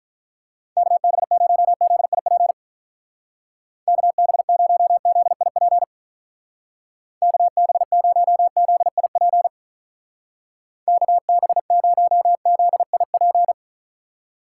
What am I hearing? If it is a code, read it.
Morse code, heard as a keyed tone, about 47 wpm, 700 Hz. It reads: KB0ZIP KB0ZIP KB0ZIP KB0ZIP